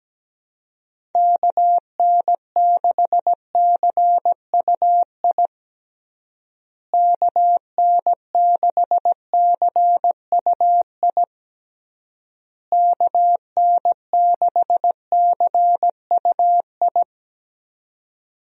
Morse code, heard as a keyed tone, about 17 wpm, 700 Hz. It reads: KN6CUI KN6CUI KN6CUI
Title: USA Call Signs - 6 Characters - Repeated 3x - 17wpm